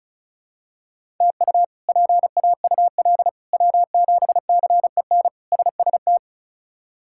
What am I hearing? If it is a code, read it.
Morse code, heard as a keyed tone, about 34 wpm, 700 Hz. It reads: TU PAUL W7CEN SST
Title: K1USN SST QSO Practice - 35wpm